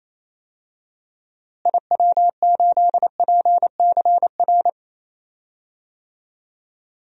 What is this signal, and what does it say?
Morse code, heard as a keyed tone, about 28 wpm, 700 Hz. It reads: IW8PCR
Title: DXCC-100 Call Signs - 6 Characters - Repeated 3x - 28wpm